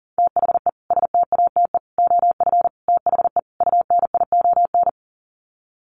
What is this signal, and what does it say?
Morse code, heard as a keyed tone, about 40 wpm, 700 Hz. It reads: THE STATE OF THE UNION
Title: Sets of 5 Words Encore 40wpm